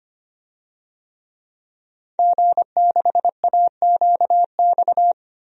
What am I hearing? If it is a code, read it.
Morse code, heard as a keyed tone, about 25 wpm, 700 Hz. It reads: G6AQX
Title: DXCC-100 Call Signs - 5 Characters - Repeated 3x - 25wpm